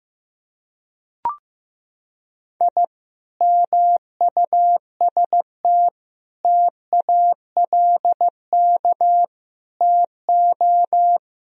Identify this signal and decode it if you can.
Morse code, heard as a keyed tone, about 16 wpm, 700 Hz. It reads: I MUST TALK TO
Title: Sentences from Top 1000 Words - Review Part 1 - 15wpm